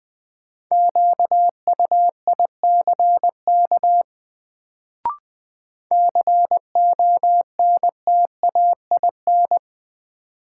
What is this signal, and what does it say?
Morse code, heard as a keyed tone, about 20 wpm, 700 Hz. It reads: QUICK CONTAIN
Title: Sentences from Top 500 Words - New Word Drill - 20wpm